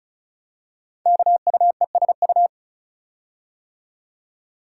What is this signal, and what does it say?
Morse code, heard as a keyed tone, about 36 wpm, 700 Hz. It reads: KUESU
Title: Code Groups - 5 Letter:Numbers - 35wpm